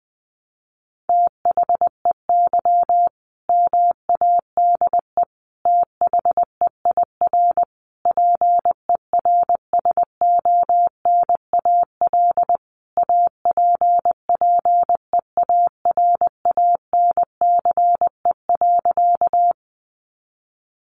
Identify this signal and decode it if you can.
Morse code, heard as a keyed tone, about 20 wpm, 700 Hz. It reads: THEY MADE THEIR PERSONAL APPEARANCE.
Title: Sentences from Top 1100 Words 20wpm